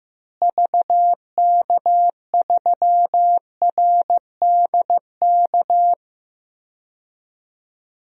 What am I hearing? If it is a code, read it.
Morse code, heard as a keyed tone, about 15 wpm, 700 Hz. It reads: VK3RDK
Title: DXCC-100 Call Signs - 6 Characters - Repeated 2x - 15wpm